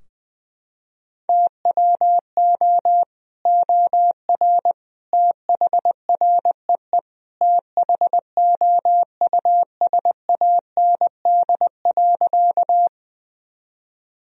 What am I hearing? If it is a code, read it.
Morse code, heard as a keyed tone, about 20 wpm, 700 Hz. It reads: TWO OR THREE THOUSAND.
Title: Sentences from Top 1000 Words - Review Part 3 - 20wpm